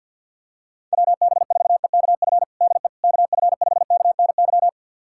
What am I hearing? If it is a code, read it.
Morse code, heard as a keyed tone, about 51 wpm, 700 Hz. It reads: WB4EKF DE KF5KNY
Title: USA Callsign DE Callsign - 6 Characters - 50wpm